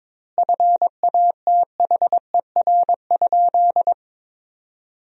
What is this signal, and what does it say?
Morse code, heard as a keyed tone, about 23 wpm, 700 Hz. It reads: FATHER?
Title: 2x Word-Spacing - Sentences from Top 400 Words - 22wpm